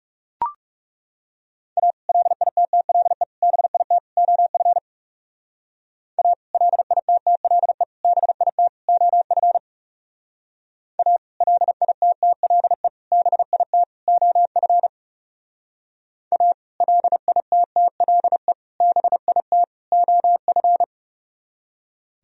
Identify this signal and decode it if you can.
Morse code, heard as a keyed tone, about 36 wpm, 700 Hz. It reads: A LITTLE BIT OF A LITTLE BIT OF A LITTLE BIT OF A LITTLE BIT OF
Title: Speed-Racing - Sets of 4 Words 30wpm